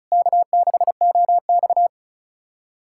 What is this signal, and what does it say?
Morse code, heard as a keyed tone, about 36 wpm, 700 Hz. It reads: K6OX
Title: USA Callsign DE Callsign - 4 Characters - 35wpm